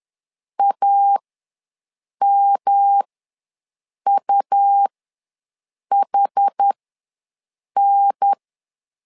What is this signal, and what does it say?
Morse code, heard as a keyed tone, about 10 wpm, spaced as for 6 wpm, 800 Hz. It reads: AMUHN
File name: LEZIONI / 3 MWRO / 3 MWR ws 10wpm 2151